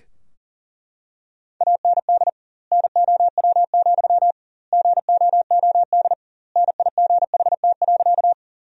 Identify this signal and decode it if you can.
Morse code, heard as a keyed tone, about 37 wpm, 700 Hz. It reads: AND NOW, GOOD NIGHT.
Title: 2x Word-Spacing - Sentences from Top 300 Words - 40wpm